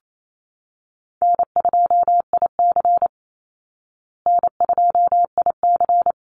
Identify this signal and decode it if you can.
Morse code, heard as a keyed tone, about 28 wpm, 700 Hz. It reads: N2IC N2IC
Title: Call Signs Repeated 3x - 28wpm